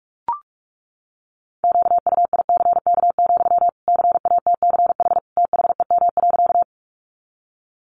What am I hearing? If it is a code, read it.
Morse code, heard as a keyed tone, about 44 wpm, 700 Hz. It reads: QUICK, CATCH THEM.
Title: Sentences from Top 1500 Words 45wpm